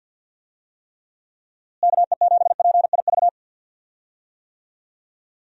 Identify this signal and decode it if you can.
Morse code, heard as a keyed tone, about 50 wpm, 700 Hz. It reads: KE7PIV
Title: USA Call Signs - 6 Characters - Repeated 3x - 50wpm